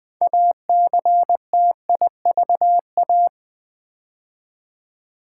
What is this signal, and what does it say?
Morse code, heard as a keyed tone, about 21 wpm, 700 Hz. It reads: ACTIVA
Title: Build A Word - 3 Letters At A Time - 8-Letter Words - 20wpm